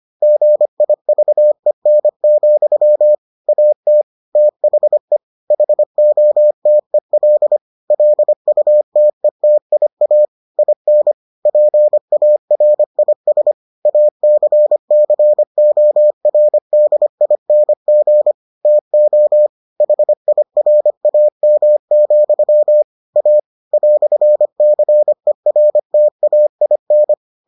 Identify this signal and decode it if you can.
Morse code, heard as a keyed tone, about 25 wpm, 600 Hz. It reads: GIVEN, AT THE HOTEL LUTETIA IN PARIS ACCORDING TO HIRAM, A "CERTAIN